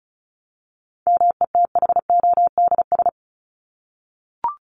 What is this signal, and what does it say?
Morse code, heard as a keyed tone, about 34 wpm, 700 Hz. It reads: METHODS
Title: ES:S Suffix - Top 100 Words with Base Words 35wpm